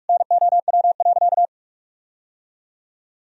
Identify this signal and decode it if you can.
Morse code, heard as a keyed tone, about 42 wpm, 700 Hz. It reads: NOW.
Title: Sentences from Top 1000 Words - Review Part 4 - 45wpm